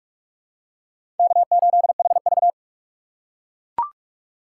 Morse code, heard as a keyed tone, about 43 wpm, 700 Hz. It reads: K8HV
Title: USA Call Signs - 4 Characters - 6x Character Spacing - 45wpm